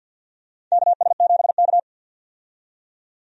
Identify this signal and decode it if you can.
Morse code, heard as a keyed tone, about 52 wpm, 700 Hz. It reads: KS7K
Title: USA Call Signs - 4 Characters - 50wpm